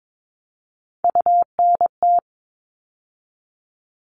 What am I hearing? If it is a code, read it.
Morse code, heard as a keyed tone, about 22 wpm, 700 Hz. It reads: UNT
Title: All Cut Numbers - 3-Digit 22wpm